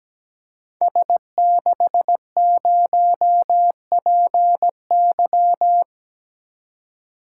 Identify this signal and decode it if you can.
Morse code, heard as a keyed tone, about 17 wpm, 700 Hz. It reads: S60PY
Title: Code Groups - 5 Letter:Numbers - 17wpm